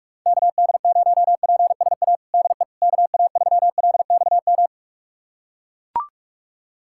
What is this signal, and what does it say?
Morse code, heard as a keyed tone, about 41 wpm, 700 Hz. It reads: KD0PSA DE KA3LXM
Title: USA Callsign DE Callsign - 6 Characters - 45wpm